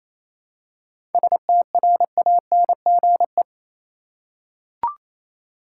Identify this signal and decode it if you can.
Morse code, heard as a keyed tone, about 28 wpm, 700 Hz. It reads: STRANGE
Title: Seven Letter Words - 28wpm